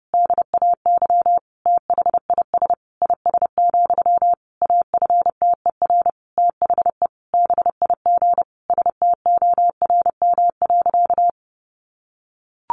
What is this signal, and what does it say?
Morse code, heard as a keyed tone, about 30 wpm, 700 Hz. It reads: DAY THIS IS, AFTER THE BIG STORM.
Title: Sentences from Top 1500 Words 30wpm